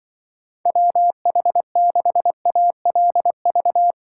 Morse code, heard as a keyed tone, about 24 wpm, 700 Hz. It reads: WH6ALV
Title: Speed Racing - USA 2x3 Call Signs 20wpm